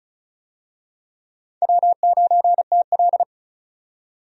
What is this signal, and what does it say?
Morse code, heard as a keyed tone, about 34 wpm, 700 Hz. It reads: W9TL
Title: USA Call Signs - 4 Characters - 35wpm